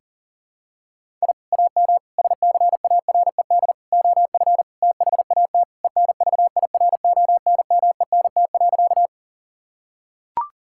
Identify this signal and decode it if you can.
Morse code, heard as a keyed tone, about 40 wpm, 700 Hz. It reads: I AM SCARED OF THAT ENVIRONMENT.
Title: Sentences from Top 1000 Words - Review Part 5 - 40wpm